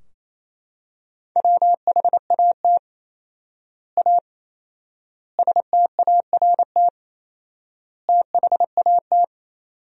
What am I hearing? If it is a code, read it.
Morse code, heard as a keyed tone, about 28 wpm, 700 Hz. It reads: WHAT A START THAT
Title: 4x Word-Spacing - Sentences from Top 300 Words - 28wpm